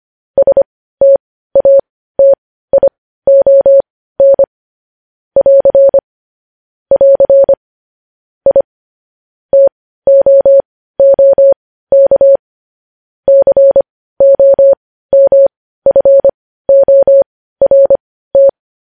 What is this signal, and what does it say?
Morse code, heard as a keyed tone, about 25 wpm, spaced as for 15 wpm, 550 Hz. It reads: STATION <AR> <AR> I TOOK COMFORT
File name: podcast / News headlines-1x-25x15